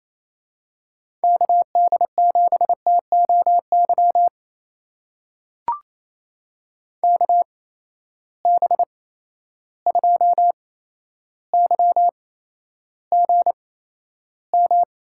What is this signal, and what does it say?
Morse code, heard as a keyed tone, about 27 wpm, spaced as for 12 wpm, 700 Hz. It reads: KD7TOY KB2YGM
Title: USA Call Signs - 6 Characters - 8x Character Spacing - 28wpm